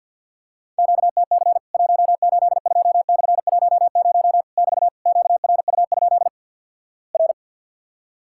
Morse code, heard as a keyed tone, about 50 wpm, 700 Hz. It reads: QTC 182/10 <BT> QRU? R
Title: Worked All Europe DX Contest - QTC Exchange with Repeats - 50wpm